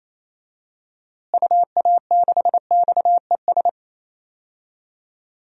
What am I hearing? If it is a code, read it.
Morse code, heard as a keyed tone, about 27 wpm, 700 Hz. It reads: UA6XES
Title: DXCC-100 Call Signs - 6 Characters - Repeated 2x - 28wpm